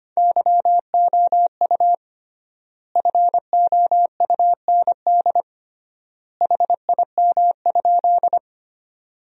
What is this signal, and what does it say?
Morse code, heard as a keyed tone, about 25 wpm, 700 Hz. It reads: YOU FOUND HIM?
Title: 3x Word-Spacing - Sentences from Top 300 Words - 25wpm